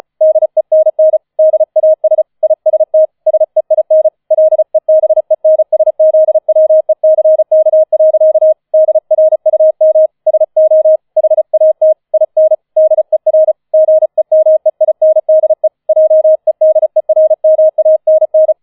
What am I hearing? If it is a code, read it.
Morse code, heard as a keyed tone, about 35 wpm, 630 Hz. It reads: DENN DAS IST SEIN LEBENSZWECK. DRUM SO HAT IN DER GEMEINDE JEDERMANN